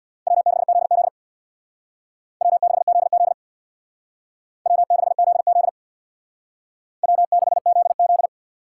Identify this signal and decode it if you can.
Morse code, heard as a keyed tone, about 64 wpm, 700 Hz. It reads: W6ZZ W6ZZ W6ZZ W6ZZ